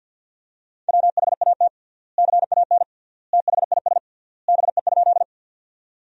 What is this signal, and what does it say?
Morse code, heard as a keyed tone, about 50 wpm, 700 Hz. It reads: WHAT CAN THIS BE?
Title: 3x Word-Spacing - Sentences from Top 100 Words - 50wpm